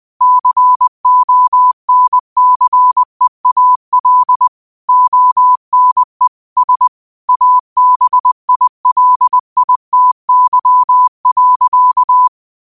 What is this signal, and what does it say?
Morse code, heard as a keyed tone, about 20 wpm, 1000 Hz. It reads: CONCEAL ONE S ABILITY.
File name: qotdfiles / 20wpm